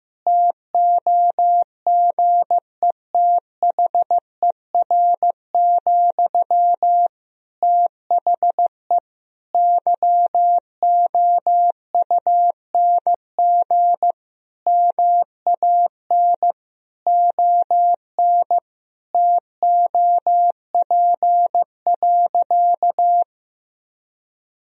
Morse code, heard as a keyed tone, about 15 wpm, 700 Hz. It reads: TOGETHER, THE YOUNG MAN ON TOP.
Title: Sentences from Top 1000 Words - Review Part 2 - 15wpm